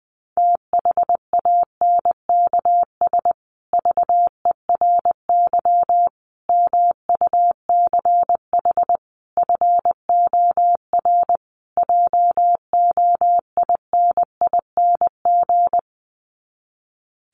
Morse code, heard as a keyed tone, about 19 wpm, 700 Hz. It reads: THANKS VERY MUCH FOR JOINING
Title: Sets of 5 Words Encore2 20wpm